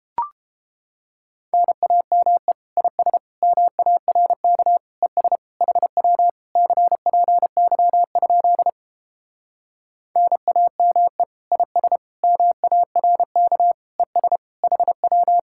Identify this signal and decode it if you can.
Morse code, heard as a keyed tone, about 29 wpm, 700 Hz. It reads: NAME IS MARK ES HW CPY? NAME IS MARK ES HW